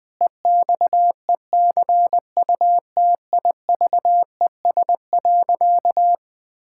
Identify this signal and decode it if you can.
Morse code, heard as a keyed tone, about 21 wpm, 700 Hz. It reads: EXECUTIVES.
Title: Pangram Challenge v2 - 20wpm